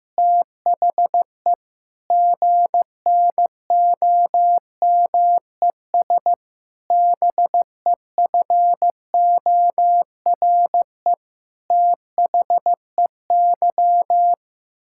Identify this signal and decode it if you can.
Morse code, heard as a keyed tone, about 15 wpm, 700 Hz. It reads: THE GNOMES BEFORE THEY
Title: Pangram Challenge v2 - 15wpm